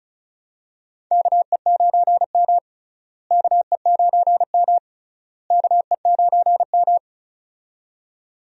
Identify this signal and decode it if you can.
Morse code, heard as a keyed tone, about 36 wpm, 700 Hz. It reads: KE9M KE9M KE9M
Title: USA Call Signs - 4 Characters - Repeated 3x - 35wpm